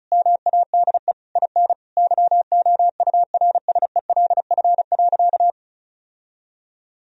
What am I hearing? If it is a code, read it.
Morse code, heard as a keyed tone, about 36 wpm, 700 Hz. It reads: MADE IN YOURSELF.
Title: Sentences from Top 1000 Words - Review Part 1 - 35wpm